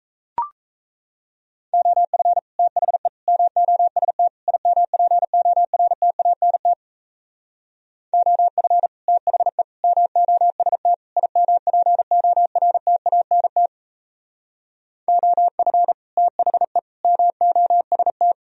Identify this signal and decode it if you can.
Morse code, heard as a keyed tone, about 33 wpm, 700 Hz. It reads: OF THE MOST IMPORTANT OF THE MOST IMPORTANT OF THE MOST
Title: Speed-Racing - Sets of 4 Words 28wpm